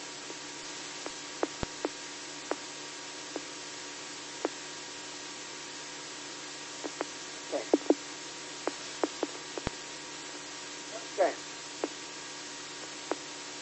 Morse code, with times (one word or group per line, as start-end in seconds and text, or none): none